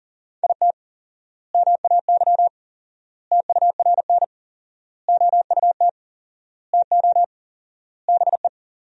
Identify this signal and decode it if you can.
Morse code, heard as a keyed tone, about 40 wpm, 700 Hz. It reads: IT MAY TURN OUT TO BE
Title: 4x Word-Spacing - Sentences from Top 400 Words - 40wpm